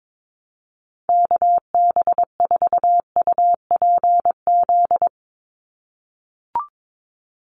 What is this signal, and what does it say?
Morse code, heard as a keyed tone, about 22 wpm, 700 Hz. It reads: KB4UPZ
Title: USA Call Signs 22wpm